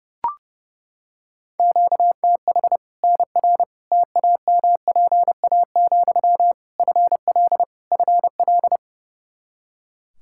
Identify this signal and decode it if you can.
Morse code, heard as a keyed tone, about 30 wpm, 700 Hz. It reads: QTH NR TAMPA, FL FL
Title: K1USN SST QSO Practice 30wpm